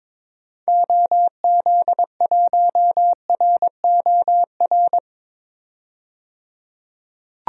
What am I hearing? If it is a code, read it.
Morse code, heard as a keyed tone, about 21 wpm, 700 Hz. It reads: OZ1ROR
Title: DXCC-100 Call Signs - 6 Characters - 22wpm